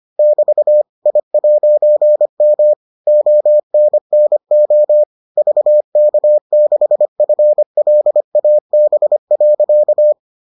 Morse code, heard as a keyed tone, about 25 wpm, 600 Hz. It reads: <BT> I'M ONNO VK6FLAB.